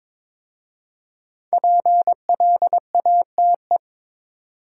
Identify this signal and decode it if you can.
Morse code, heard as a keyed tone, about 22 wpm, 700 Hz. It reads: PLATE
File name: 5 Letter Words - First 3 Letters Given - 22wpm